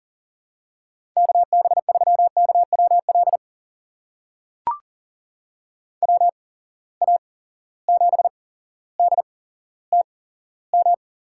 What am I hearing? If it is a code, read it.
Morse code, heard as a keyed tone, about 40 wpm, spaced as for 17 wpm, 700 Hz. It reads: KB3KWL WA7DTM
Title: USA Call Signs - 6 Characters - 8x Character Spacing - 40wpm